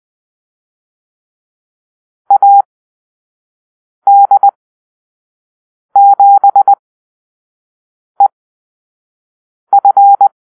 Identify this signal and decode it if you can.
Morse code, heard as a keyed tone, about 20 wpm, spaced as for 5 wpm, 800 Hz. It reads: AD7EF
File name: Call-Sign Drill